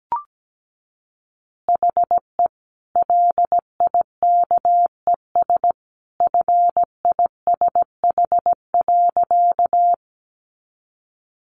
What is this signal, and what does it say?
Morse code, heard as a keyed tone, about 17 wpm, 700 Hz. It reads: HE LIKES FISH.
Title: Easy Phrases 17wpm